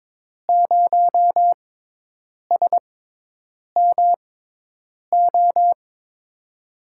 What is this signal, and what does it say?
Morse code, heard as a keyed tone, about 22 wpm, spaced as for 8 wpm, 700 Hz. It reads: 0SMO
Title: USA Call Signs - 6 Characters - 6x Character Spacing - 22wpm